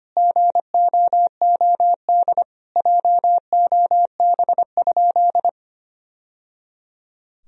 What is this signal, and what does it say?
Morse code, heard as a keyed tone, about 25 wpm, 700 Hz. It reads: GOOD JOB?
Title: Sentences from Top 1000 Words - Review Part 3 - 25wpm